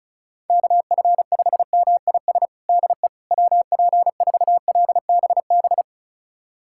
Morse code, heard as a keyed tone, about 34 wpm, 690 Hz. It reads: KF5MIS DE WP4LBB
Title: USA Callsign DE Callsign - 6 Characters - 35wpm